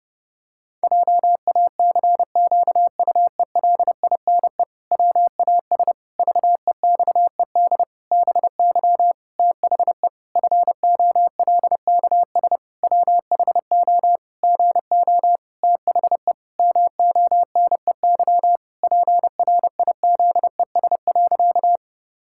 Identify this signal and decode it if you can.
Morse code, heard as a keyed tone, about 30 wpm, 700 Hz. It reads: JACQUELINE WAS VEXED BY THE FOLKS WHO GO THE MONEY PRIZES.
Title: Pangram Challenge v2 - 30wpm